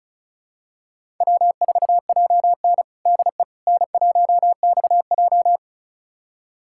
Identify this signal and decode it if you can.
Morse code, heard as a keyed tone, about 36 wpm, 700 Hz. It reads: W4JN DE N1XJ
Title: USA Callsign DE Callsign - 4 Characters - 35wpm